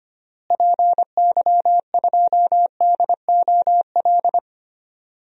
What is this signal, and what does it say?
Morse code, heard as a keyed tone, about 25 wpm, 700 Hz. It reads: PY2DOL
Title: DXCC-100 Call Signs - 6 Characters - Repeated 2x - 25wpm